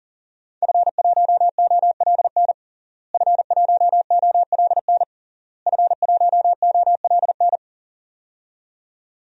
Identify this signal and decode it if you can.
Morse code, heard as a keyed tone, about 40 wpm, 700 Hz. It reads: F1OLN F1OLN F1OLN
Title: DXCC-100 Call Signs - 5 Characters - Repeated 3x - 40wpm